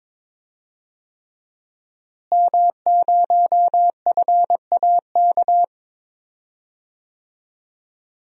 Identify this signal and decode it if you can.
Morse code, heard as a keyed tone, about 22 wpm, 700 Hz. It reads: M0FAK
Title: DXCC-100 Call Signs - 5 Characters - Repeated 3x - 22wpm